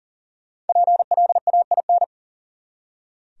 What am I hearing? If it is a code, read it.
Morse code, heard as a keyed tone, about 52 wpm, 700 Hz. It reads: PLAIN